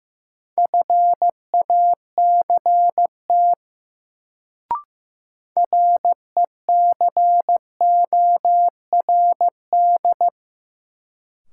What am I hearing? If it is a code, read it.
Morse code, heard as a keyed tone, about 15 wpm, 700 Hz. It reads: FACT RECORD
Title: Sentences from Top 500 Words - New Word Drill - 15wpm